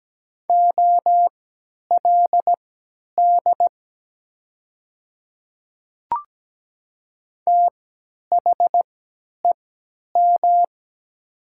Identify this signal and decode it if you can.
Morse code, heard as a keyed tone, about 17 wpm, spaced as for 10 wpm, 700 Hz. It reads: OLD THEM
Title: 3x Character Spacing - Top 200 Words - 17wpm